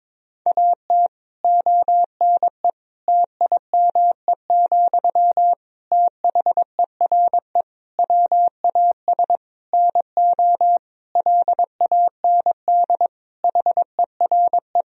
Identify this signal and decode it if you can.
Morse code, heard as a keyed tone, about 22 wpm, 700 Hz. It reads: AT ONE TIME, THERE WAS NO LAND HERE
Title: Sentences from Top 1000 Words - Review Part 1 - 22wpm